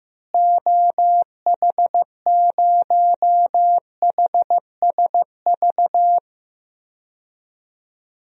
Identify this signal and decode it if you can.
Morse code, heard as a keyed tone, about 15 wpm, 700 Hz. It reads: OH0HSV
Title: DXCC-100 Call Signs - 6 Characters - Repeated 2x - 15wpm